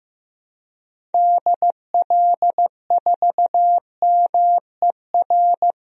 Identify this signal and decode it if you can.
Morse code, heard as a keyed tone, about 15 wpm, 700 Hz. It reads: DL4MER
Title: DXCC-100 Call Signs - 6 Characters - Repeated 2x - 15wpm